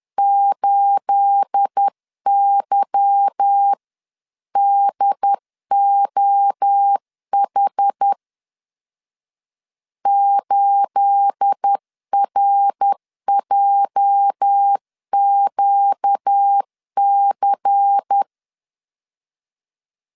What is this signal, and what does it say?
Morse code, heard as a keyed tone, about 10 wpm, 800 Hz. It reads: 8YDOH 8RJQC
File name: LEZIONI / 9 28 10wpm 2842